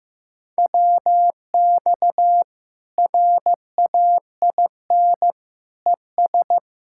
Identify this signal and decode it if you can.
Morse code, heard as a keyed tone, about 16 wpm, 700 Hz. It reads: WX RAIN ES
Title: K1USN SST QSO Practice 15wpm